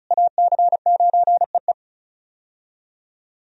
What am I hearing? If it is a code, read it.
Morse code, heard as a keyed tone, about 34 wpm, 700 Hz. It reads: AC9EE
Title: USA Call Signs - 5 Characters - Repeated 3x - 35wpm